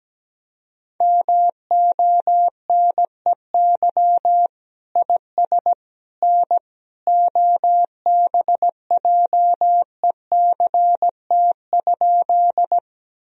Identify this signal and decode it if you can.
Morse code, heard as a keyed tone, about 17 wpm, 700 Hz. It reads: MONEY IS N OBJECT?
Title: Sentences from Top 500 Words and Abbreviations 17wpm